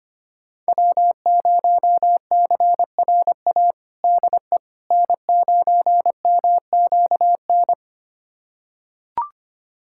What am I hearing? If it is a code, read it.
Morse code, heard as a keyed tone, about 25 wpm, 700 Hz. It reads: W0CRA DE N9MQN
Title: USA Callsign DE Callsign - 5 Characters - 25wpm